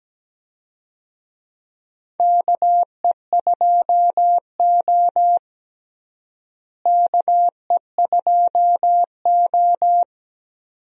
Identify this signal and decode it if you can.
Morse code, heard as a keyed tone, about 17 wpm, 700 Hz. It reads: KE2O KE2O
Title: USA Call Signs - 4 Characters - Repeated 2x - 17wpm